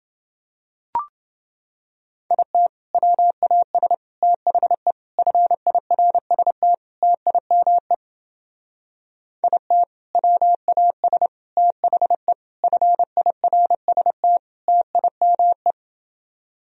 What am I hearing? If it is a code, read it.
Morse code, heard as a keyed tone, about 28 wpm, 700 Hz. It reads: IT WAS THE FIRST TIME IT WAS THE FIRST TIME